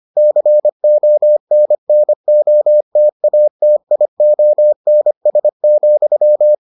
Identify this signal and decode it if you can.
Morse code, heard as a keyed tone, about 25 wpm, 600 Hz. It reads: CONNOTATIONS,